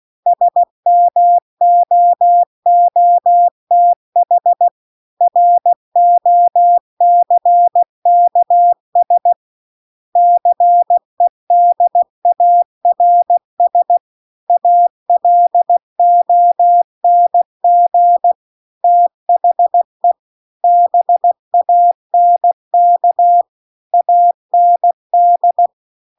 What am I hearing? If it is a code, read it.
Morse code, heard as a keyed tone, about 16 wpm, 700 Hz. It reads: SMOOTH ROCKS CEDARS ALONG THE BANK AND